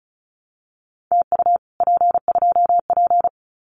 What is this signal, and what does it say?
Morse code, heard as a keyed tone, about 34 wpm, 700 Hz. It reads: TU P2P
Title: Common QSO Phrases Encore 35wpm